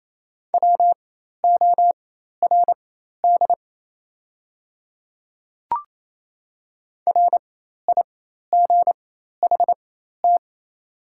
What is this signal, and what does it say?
Morse code, heard as a keyed tone, about 28 wpm, spaced as for 13 wpm, 700 Hz. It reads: WORD RIGHT
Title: 4x Character Spacing - Top 500 Words - 28wpm